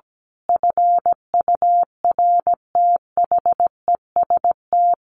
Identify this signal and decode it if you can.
Morse code, heard as a keyed tone, about 17 wpm, 700 Hz. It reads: FURTHEST
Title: A Sound Of Thunder 17wpm